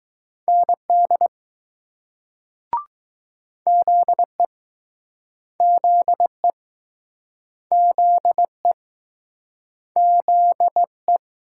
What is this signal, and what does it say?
Morse code, heard as a keyed tone, about 20 wpm, 700 Hz. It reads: ND ZE ZE ZE ZE